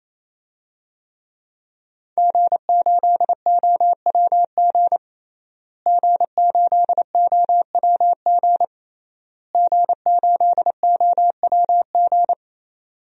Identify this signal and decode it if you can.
Morse code, heard as a keyed tone, about 28 wpm, 700 Hz. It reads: G8OWG G8OWG G8OWG
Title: DXCC-100 Call Signs - 5 Characters - Repeated 3x - 28wpm